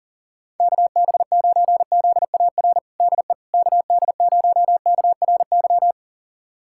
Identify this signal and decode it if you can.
Morse code, heard as a keyed tone, about 37 wpm, 690 Hz. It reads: KB9ZAR DE KD0KRY
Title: USA Callsign DE Callsign - 6 Characters - 40wpm